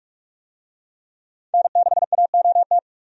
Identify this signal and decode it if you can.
Morse code, heard as a keyed tone, about 46 wpm, 700 Hz. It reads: N6AOT
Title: USA Call Signs - 5 Characters - 45wpm